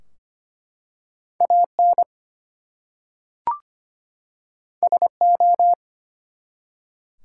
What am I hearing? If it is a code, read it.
Morse code, heard as a keyed tone, about 25 wpm, 700 Hz. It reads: AN SO
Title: Two Letter Words - 25wpm